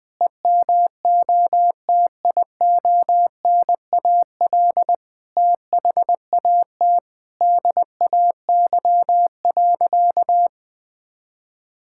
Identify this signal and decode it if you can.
Morse code, heard as a keyed tone, about 20 wpm, 700 Hz. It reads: EMOTIONAL THAT DAY.
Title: Sentences from Top 1000 Words - Review Part 5 - 20wpm